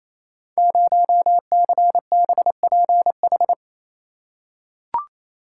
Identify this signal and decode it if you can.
Morse code, heard as a keyed tone, about 28 wpm, 700 Hz. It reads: 0CBPH
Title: Code Groups - 5 Letter:Numbers - 28wpm